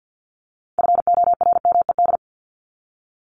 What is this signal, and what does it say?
Morse code, heard as a keyed tone, about 50 wpm, 700 Hz. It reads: FORMED
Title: ED Suffix - Top 100 Words with Base Words 50wpm